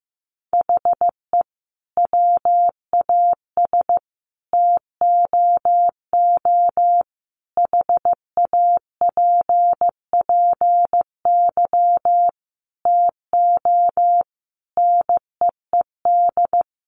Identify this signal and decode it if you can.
Morse code, heard as a keyed tone, about 15 wpm, 700 Hz. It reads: HE WAS TOO HAPPY TO NEED
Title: Sentences from Top 1000 Words 15wpm